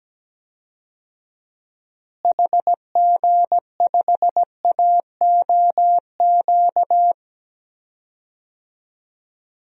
Morse code, heard as a keyed tone, about 17 wpm, 700 Hz. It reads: HG5AOQ
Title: DXCC-100 Call Signs - 6 Characters - Repeated 3x - 17wpm